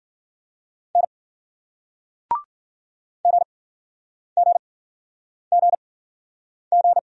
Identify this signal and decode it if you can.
Morse code, heard as a keyed tone, about 54 wpm, spaced as for 9 wpm, 700 Hz. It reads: N GGGG